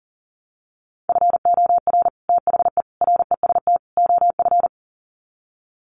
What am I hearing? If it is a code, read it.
Morse code, heard as a keyed tone, about 40 wpm, 700 Hz. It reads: FOR THE REST OF
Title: Sets of 4 Words 40wpm